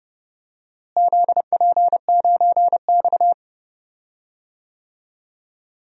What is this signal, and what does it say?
Morse code, heard as a keyed tone, about 30 wpm, 700 Hz. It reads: ZP9X
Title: DXCC-100 Call Signs - 4 Characters - Repeated 3x - 30wpm